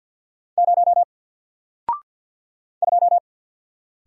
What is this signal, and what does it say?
Morse code, heard as a keyed tone, about 50 wpm, 700 Hz. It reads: 0 2
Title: Single Character (Letter:Number:Punctuation:Prosign) 50wpm